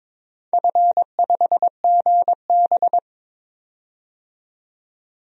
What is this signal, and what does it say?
Morse code, heard as a keyed tone, about 21 wpm, 700 Hz. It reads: F5GB
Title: DXCC-100 Call Signs - 4 Characters - 22wpm